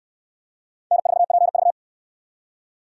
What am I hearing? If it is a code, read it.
Morse code, heard as a keyed tone, about 63 wpm, 700 Hz. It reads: N4PV